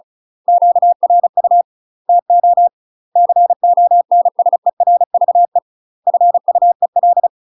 Morse code, heard as a keyed tone, about 34 wpm, 700 Hz. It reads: QRU TO CONSERVE FUEL
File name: UK MASTER QSOs TRAINING/35WPM